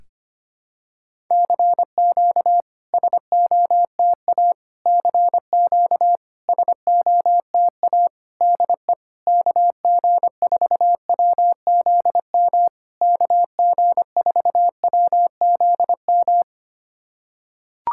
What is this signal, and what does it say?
Morse code, heard as a keyed tone, about 25 wpm, 700 Hz. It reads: CQ SOTA CQ SOTA DE KG4WZM KG4WZM
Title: SOTA QSO Practice 25wpm